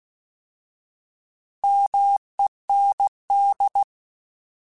Morse code, heard as a keyed tone, about 16 wpm, 800 Hz. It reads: MEND
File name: Alphabetical / Alphabetical mp3s / Ms @15